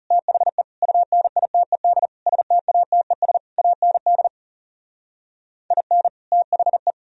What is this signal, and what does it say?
Morse code, heard as a keyed tone, about 34 wpm, 700 Hz. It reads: THE UNITED STATES AND IN THE
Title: Speed-Racing - Sets of 5 Words 30wpm